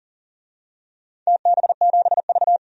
Morse code, heard as a keyed tone, about 37 wpm, 700 Hz. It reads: TB7V
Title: DXCC-100 Call Signs - 4 Characters - 40wpm